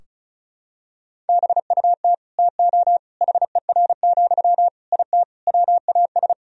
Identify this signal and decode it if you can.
Morse code, heard as a keyed tone, about 34 wpm, 700 Hz. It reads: BUT TO HER, IT WAS
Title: Sentences from Top 1000 Words - Review Part 2 - 35wpm